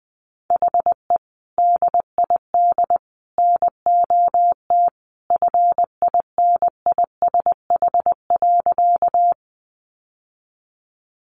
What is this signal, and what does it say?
Morse code, heard as a keyed tone, about 19 wpm, 700 Hz. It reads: HE DID NOT FINISH.
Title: Sentences from Top 1200 Words 20wpm